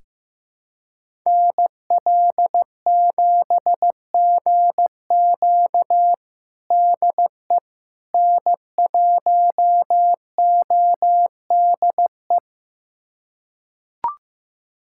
Morse code, heard as a keyed tone, about 15 wpm, 700 Hz. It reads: NL7GQ DE N1ODE
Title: USA Callsign DE Callsign - 5 Characters - 15wpm